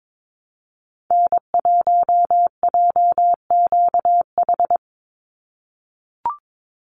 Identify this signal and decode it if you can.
Morse code, heard as a keyed tone, about 22 wpm, 700 Hz. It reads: N1JQH
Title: USA Call Signs 22wpm